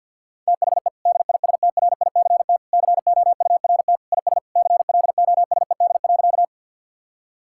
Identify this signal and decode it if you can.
Morse code, heard as a keyed tone, about 50 wpm, 700 Hz. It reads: THE DISTRICT COURT IS CLOSED.
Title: Sentences from Top 1600 Words 50wpm